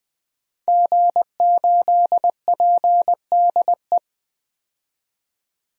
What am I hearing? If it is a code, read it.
Morse code, heard as a keyed tone, about 20 wpm, 700 Hz. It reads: G8PDE
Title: DXCC-100 Call Signs - 5 Characters - 20wpm